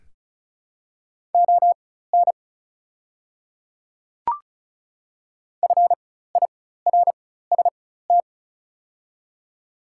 Morse code, heard as a keyed tone, about 36 wpm, spaced as for 16 wpm, 700 Hz. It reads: ON FIRST
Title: 4x Character Spacing - Top 100 Words - 35wpm